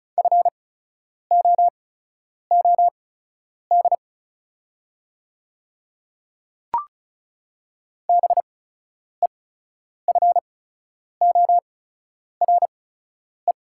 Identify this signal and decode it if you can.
Morse code, heard as a keyed tone, about 36 wpm, spaced as for 10 wpm, 690 Hz. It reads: FOOD BEFORE
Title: 8x Character Spacing - Top 300 Words - 35wpm